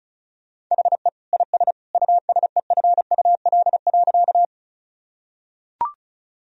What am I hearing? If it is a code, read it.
Morse code, heard as a keyed tone, about 36 wpm, 700 Hz. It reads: HE IS USEFUL.
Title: Sentences from Top 1000 Words - Review Part 4 - 35wpm